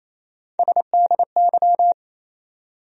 Morse code, heard as a keyed tone, about 28 wpm, 700 Hz. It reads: SDY
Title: USA Call Sign Postfixes - 3 Letters 28wpm